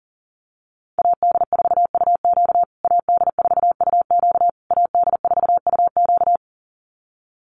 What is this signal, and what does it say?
Morse code, heard as a keyed tone, about 40 wpm, 700 Hz. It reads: AD4UQ AD4UQ AD4UQ
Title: USA Call Signs Repeated 3x 40wpm